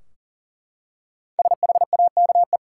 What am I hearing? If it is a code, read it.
Morse code, heard as a keyed tone, about 40 wpm, 700 Hz. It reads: SHAKE
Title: Five Letter Words - 40wpm